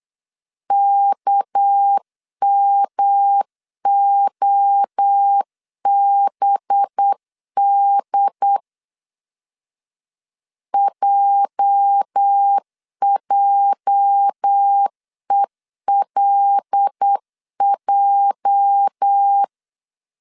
Morse code, heard as a keyed tone, about 8 wpm, 800 Hz. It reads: KMOBD JJELJ
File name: LEZIONI / 8 YJ45 / 8 YJ 8wpm 3313